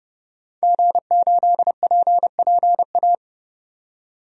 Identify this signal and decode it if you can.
Morse code, heard as a keyed tone, about 30 wpm, 700 Hz. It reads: G8PPA
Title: DXCC-100 Call Signs - 5 Characters - 30wpm